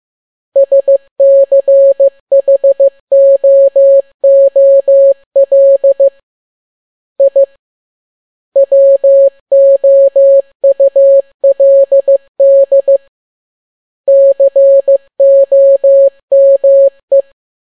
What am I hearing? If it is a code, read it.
Morse code, heard as a keyed tone, about 15 wpm, 550 Hz. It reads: SCHOOL I WOULD COME
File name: n6tt p1 15C13